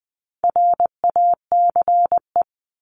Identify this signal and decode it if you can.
Morse code, heard as a keyed tone, about 20 wpm, 700 Hz. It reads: RACE
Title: Top 1501-1750 Words 20wpm